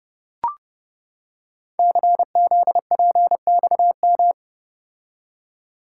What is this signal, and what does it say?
Morse code, heard as a keyed tone, about 28 wpm, 700 Hz. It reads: CZPXM
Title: Code Groups - 5 Letters - 30wpm